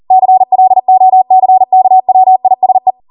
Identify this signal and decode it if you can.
Morse code, heard as a keyed tone, about 40 wpm, 750 Hz. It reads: CLOCKWISE
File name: Morse / 40WPM